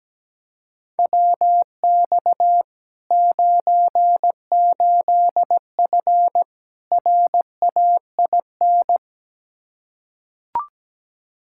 Morse code, heard as a keyed tone, about 17 wpm, 700 Hz. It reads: WX 98F RAIN
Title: Practice QSO - Exchange 2 - Info - 17wpm